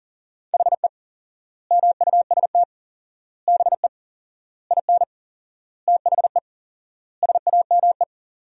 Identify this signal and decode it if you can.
Morse code, heard as a keyed tone, about 40 wpm, 700 Hz. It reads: HE MUST BE IN THE SAME
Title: 4x Word-Spacing - Sentences from Top 500 Words - 40wpm